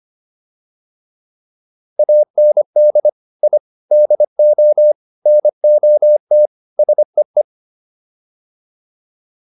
Morse code, heard as a keyed tone, about 25 wpm, 600 Hz. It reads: AND I DO NOT SEE